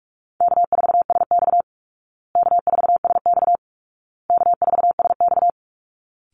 Morse code, heard as a keyed tone, about 44 wpm, 700 Hz. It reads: K4SX K4SX K4SX
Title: Easy Call Signs Repeated 3x - 45wpm